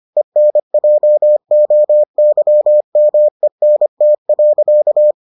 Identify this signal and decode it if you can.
Morse code, heard as a keyed tone, about 25 wpm, 600 Hz. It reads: ENJOYMENT.